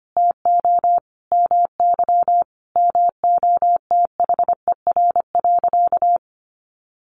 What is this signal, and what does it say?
Morse code, heard as a keyed tone, about 25 wpm, 700 Hz. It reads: TO MY MOTHER.
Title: Sentences from Top 1400 Words 25wpm